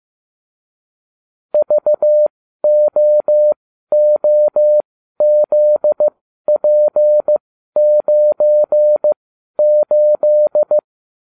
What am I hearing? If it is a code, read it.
Morse code, heard as a keyed tone, about 15 wpm, spaced as for 12 wpm, 600 Hz. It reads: VOOZP98